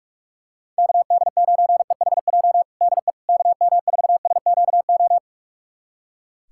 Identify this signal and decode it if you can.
Morse code, heard as a keyed tone, about 46 wpm, 700 Hz. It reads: KD9EHJ DE KM4SQO
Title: USA Callsign DE Callsign 45wpm